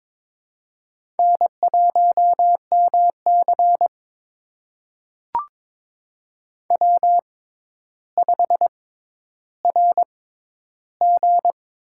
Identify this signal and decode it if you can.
Morse code, heard as a keyed tone, about 22 wpm, spaced as for 11 wpm, 700 Hz. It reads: N1MC W5RG
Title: USA Call Signs - 4 Characters - 6x Character Spacing - 22wpm